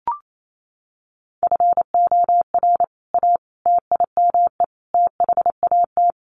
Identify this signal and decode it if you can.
Morse code, heard as a keyed tone, about 28 wpm, 700 Hz. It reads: FOR A TIME THAT
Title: Sentences from Top 700 Words 28wpm